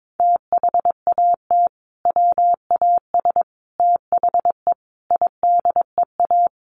Morse code, heard as a keyed tone, about 22 wpm, 700 Hz. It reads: THAT WAS THE IDEA
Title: Sentences from Top 1100 Words 22wpm